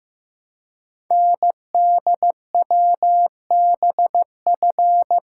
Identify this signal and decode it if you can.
Morse code, heard as a keyed tone, about 15 wpm, 700 Hz. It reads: NDWBF
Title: Code Groups - 5 Letters - 15wpm